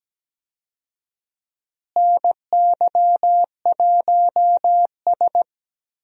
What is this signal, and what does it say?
Morse code, heard as a keyed tone, about 17 wpm, 700 Hz. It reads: NY1S
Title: USA Call Signs - 4 Characters - Repeated 3x - 17wpm